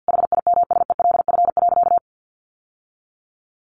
Morse code, heard as a keyed tone, about 50 wpm, 700 Hz. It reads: HIMSELF.
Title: Sentences from Top 1500 Words 50wpm